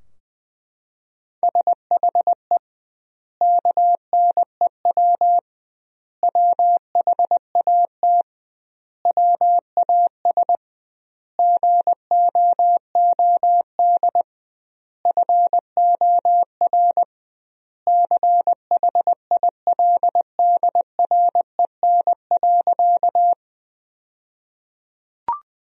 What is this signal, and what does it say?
Morse code, heard as a keyed tone, about 20 wpm, 700 Hz. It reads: SHE KNEW WHAT WAS GOOD FOR CHILDREN.
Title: 2x Word-Spacing - Sentences from Top 400 Words - 20wpm